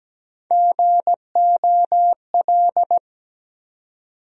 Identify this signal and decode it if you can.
Morse code, heard as a keyed tone, about 17 wpm, 700 Hz. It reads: GOL
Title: USA Call Sign Postfixes - 3 Letters 17wpm